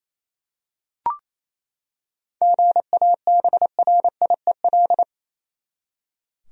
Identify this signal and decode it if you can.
Morse code, heard as a keyed tone, about 28 wpm, 700 Hz. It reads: GABRIEL
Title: USA Common Names - 28wpm